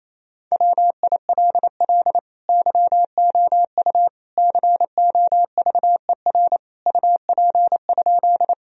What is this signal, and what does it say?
Morse code, heard as a keyed tone, about 28 wpm, 700 Hz. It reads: WILL YOU COVER UP?
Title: Sentences from Top 1000 Words - Review Part 1 - 28wpm